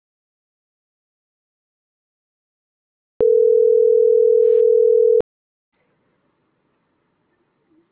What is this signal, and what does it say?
Ringback tone (North America), 440 + 480 Hz.